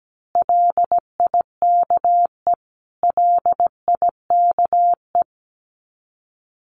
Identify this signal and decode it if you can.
Morse code, heard as a keyed tone, about 17 wpm, 700 Hz. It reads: LIKE LIKE
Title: Top 200 Words Repeated 17wpm